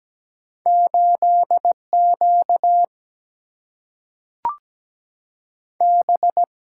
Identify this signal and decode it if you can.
Morse code, heard as a keyed tone, about 17 wpm, 700 Hz. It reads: 8 Q B
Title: Single-Letter Single-Number 17wpm